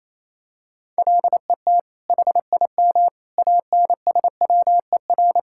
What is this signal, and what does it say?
Morse code, heard as a keyed tone, about 28 wpm, 700 Hz. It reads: LET HIM ANSWER
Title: Sentences from Top 1000 Words - Review Part 5 - 28wpm